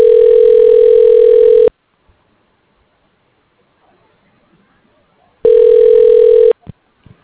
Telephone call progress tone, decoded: ringback tone

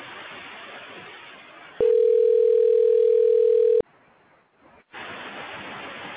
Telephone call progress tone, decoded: ringback tone

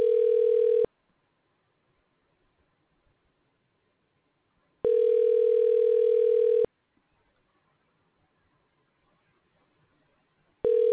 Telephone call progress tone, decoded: ringback tone

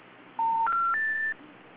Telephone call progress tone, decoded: special information tone

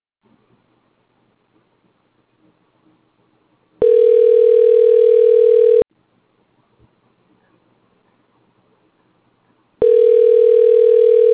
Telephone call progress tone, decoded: ringback tone

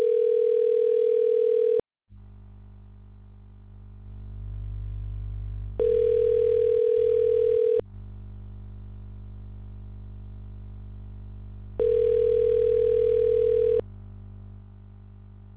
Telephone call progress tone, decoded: ringback tone

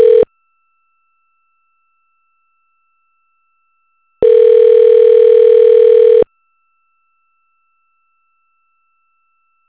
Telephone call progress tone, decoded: ringback tone

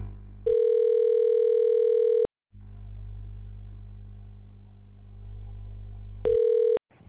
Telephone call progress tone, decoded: ringback tone

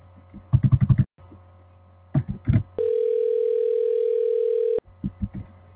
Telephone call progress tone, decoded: ringback tone

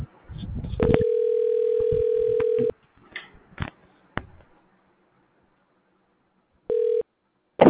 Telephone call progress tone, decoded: ringback tone

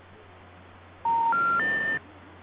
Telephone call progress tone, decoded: special information tone

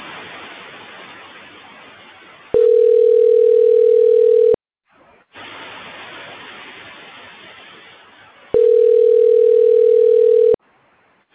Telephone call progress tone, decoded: ringback tone

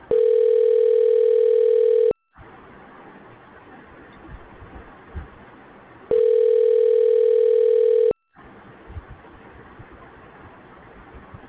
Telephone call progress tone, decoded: ringback tone

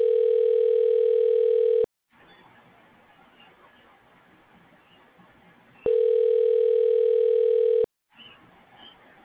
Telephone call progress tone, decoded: ringback tone

